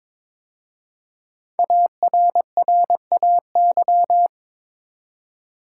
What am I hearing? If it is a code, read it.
Morse code, heard as a keyed tone, about 22 wpm, 700 Hz. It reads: ARRAY
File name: 5 Letter Words - First 3 Letters Given - 22wpm